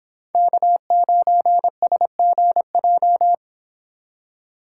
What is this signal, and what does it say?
Morse code, heard as a keyed tone, about 26 wpm, 700 Hz. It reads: K9SGJ